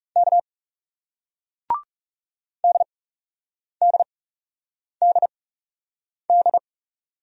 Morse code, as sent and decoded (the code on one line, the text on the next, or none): -.- / -.. -.. -.. -..
K DDDD